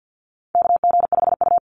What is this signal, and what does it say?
Morse code, heard as a keyed tone, about 50 wpm, 700 Hz. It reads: KG5U
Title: Call Signs - 50wpm